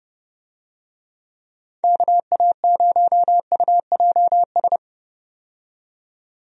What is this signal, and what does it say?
Morse code, heard as a keyed tone, about 31 wpm, 700 Hz. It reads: KA0UJS
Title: USA Call Signs - 6 Characters - 30wpm